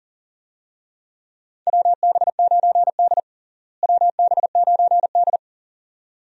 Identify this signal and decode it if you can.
Morse code, heard as a keyed tone, about 37 wpm, 700 Hz. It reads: WB9D WB9D
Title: USA Call Signs - 4 Characters - Repeated 2x - 40wpm